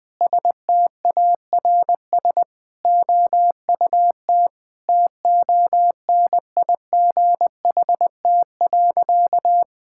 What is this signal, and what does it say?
Morse code, heard as a keyed tone, about 20 wpm, 700 Hz. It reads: STARS OUT TONIGHT.
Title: Sentences with 5-Word Phrases - 20wpm